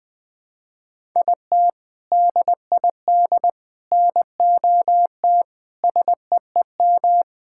Morse code, heard as a keyed tone, about 20 wpm, 700 Hz. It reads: IT DID NOT SEEM
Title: Sentences with 4-Word Phrases - 20wpm